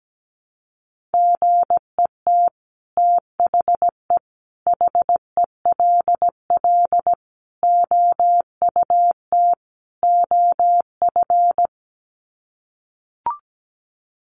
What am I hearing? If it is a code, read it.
Morse code, heard as a keyed tone, about 17 wpm, 700 Hz. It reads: GET THE HELL OUT OF
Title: Sets of 5 Words Encore2 17wpm